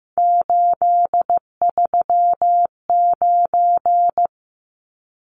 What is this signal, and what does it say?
Morse code, heard as a keyed tone, about 15 wpm, 700 Hz. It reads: 839
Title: Numbers - 3-Digit 15wpm